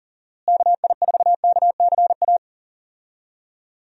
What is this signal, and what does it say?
Morse code, heard as a keyed tone, about 40 wpm, 700 Hz. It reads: KI4KCA
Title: USA Call Signs - 6 Characters - Repeated 3x - 40wpm